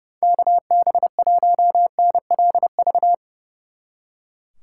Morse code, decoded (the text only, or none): KB1NLV